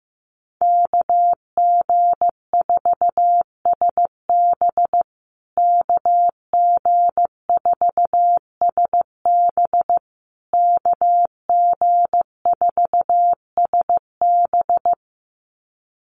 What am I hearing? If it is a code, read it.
Morse code, heard as a keyed tone, about 15 wpm, 700 Hz. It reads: KG4SB KG4SB KG4SB
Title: USA Call Signs Repeated 3x 15wpm